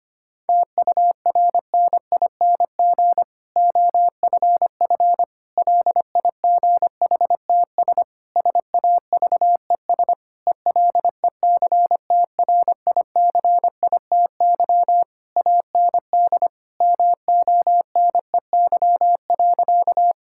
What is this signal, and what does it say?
Morse code, heard as a keyed tone, about 25 wpm, 700 Hz. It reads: TURNING OFF LIGHTS SAVES ELECTRICITY AND MONEY.
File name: practice / 25wpm / Difficult Sentences with Hints - 25wpm